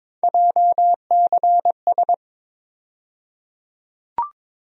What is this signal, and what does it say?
Morse code, heard as a keyed tone, about 22 wpm, 700 Hz. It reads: JCS
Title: USA Call Sign Postfixes - 3 Letters 22wpm